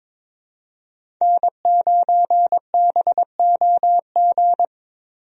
Morse code, heard as a keyed tone, about 22 wpm, 700 Hz. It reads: N9BOG
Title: USA Call Signs - 5 Characters - Repeated 3x - 22wpm